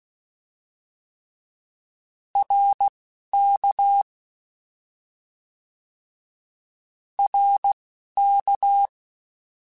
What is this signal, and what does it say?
Morse code, heard as a keyed tone, about 16 wpm, spaced as for 11 wpm, 800 Hz. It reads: RK RK
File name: Short Runs / Short Runs-mp3s / ShortContrasts @10